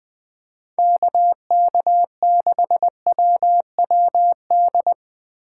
Morse code, heard as a keyed tone, about 20 wpm, 700 Hz. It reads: KK6WWD